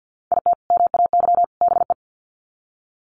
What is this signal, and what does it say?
Morse code, heard as a keyed tone, about 48 wpm, 700 Hz. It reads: IT MAY BE
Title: Sets of 3 Words 50wpm